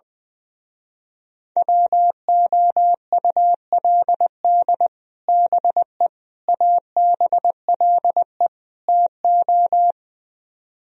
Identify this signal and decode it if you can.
Morse code, heard as a keyed tone, about 21 wpm, 700 Hz. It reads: WOULD BE ABLE TO
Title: Speed-Racing - Sets of 4 Words 20wpm